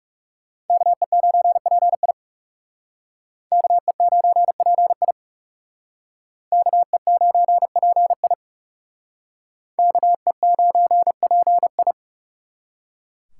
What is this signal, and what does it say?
Morse code, heard as a keyed tone, about 37 wpm, 700 Hz. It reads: KE9PI KE9PI KE9PI KE9PI
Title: Speed Racing - USA 2x2 Call Signs 30wpm